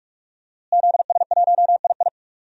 Morse code, heard as a keyed tone, about 46 wpm, 700 Hz. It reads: ZS1II
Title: DXCC-100 Call Signs - 5 Characters - Repeated 3x - 45wpm